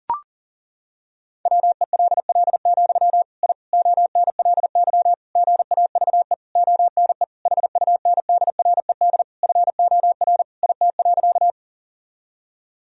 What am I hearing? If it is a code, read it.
Morse code, heard as a keyed tone, about 40 wpm, 700 Hz. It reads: WELL, I ONLY GAVE ONE HUNDRED FOR IT.
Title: Sentences from Top 1000 Words - Review Part 2 - 40wpm